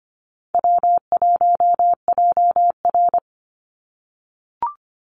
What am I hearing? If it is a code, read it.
Morse code, heard as a keyed tone, about 25 wpm, 700 Hz. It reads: W1JR
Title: Call Signs - 25wpm